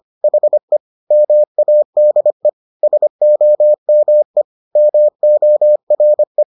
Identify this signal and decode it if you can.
Morse code, heard as a keyed tone, about 25 wpm, 600 Hz. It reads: HE MADE SOME MORE